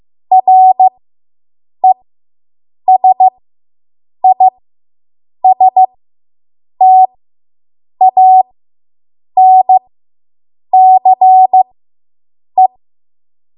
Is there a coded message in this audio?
Yes: Morse code from 0.3 to 12.7 s, heard as a keyed tone, about 15 wpm, spaced as for 7 wpm, 750 Hz. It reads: RESISTANCE